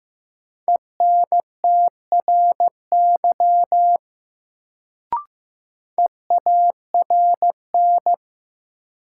Morse code, heard as a keyed tone, about 15 wpm, 700 Hz. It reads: ENTRY EARN
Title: Sentences from Top 1500 Words - New Word Drill - 15wpm